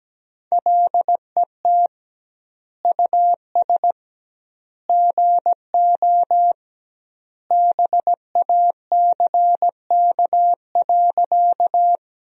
Morse code, heard as a keyed tone, about 17 wpm, 700 Hz. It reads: LET US GO BACK.
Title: 2x Word-Spacing - Sentences from Top 300 Words - 17wpm